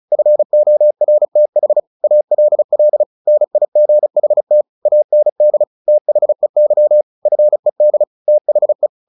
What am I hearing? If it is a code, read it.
Morse code, heard as a keyed tone, about 34 wpm, 600 Hz. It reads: FORTH ALL NIGHT AND THEY FED THE
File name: the old man and the sea 2 35wpm